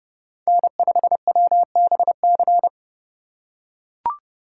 Morse code, heard as a keyed tone, about 29 wpm, 700 Hz. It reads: N5WBC